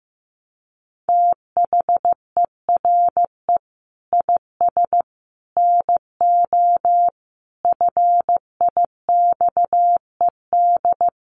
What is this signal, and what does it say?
Morse code, heard as a keyed tone, about 14 wpm, 700 Hz. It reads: THERE IS NO FIXED
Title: Sentences from Top 1100 Words 15wpm